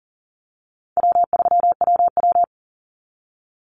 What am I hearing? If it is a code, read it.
Morse code, heard as a keyed tone, about 40 wpm, 700 Hz. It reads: W3WW
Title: Call Signs Repeated 3x - 40wpm